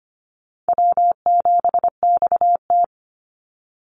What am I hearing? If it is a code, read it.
Morse code, heard as a keyed tone, about 25 wpm, 700 Hz. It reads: W7XT
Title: Easy Call Signs Repeated 2x - 25wpm